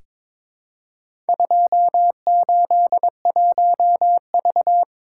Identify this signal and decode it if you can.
Morse code, heard as a keyed tone, about 22 wpm, 700 Hz. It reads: 281V
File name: practice / 22wpm / Four Letter-Number Combinations - 22wpm